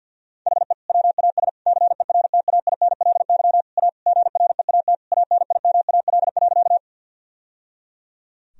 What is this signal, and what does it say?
Morse code, heard as a keyed tone, about 50 wpm, 700 Hz. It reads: HE WAS CERTAINLY A GREAT ANIMAL.
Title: Sentences from Top 1000 Words - Review Part 1 - 50wpm